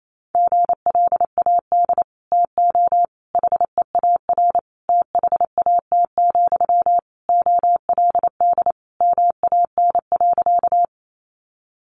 Morse code, heard as a keyed tone, about 28 wpm, 700 Hz. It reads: GLAD TO HEAR THAT, OLD MAN.
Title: Sentences from Top 800 Words 28wpm